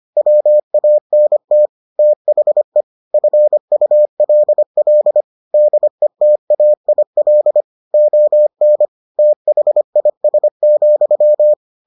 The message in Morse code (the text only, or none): WANT THE FULL DETAIL ON THIS,